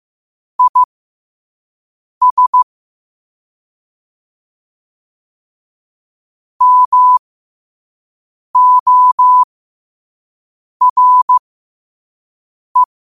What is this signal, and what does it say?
Morse code, heard as a keyed tone, about 15 wpm, spaced as for 5 wpm, 1000 Hz. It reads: IS MORE
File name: qotdfiles / 5wpm